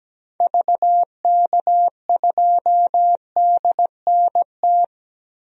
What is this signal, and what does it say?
Morse code, heard as a keyed tone, about 17 wpm, 700 Hz. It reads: VK2DNT
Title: DXCC-100 Call Signs - 6 Characters - Repeated 3x - 17wpm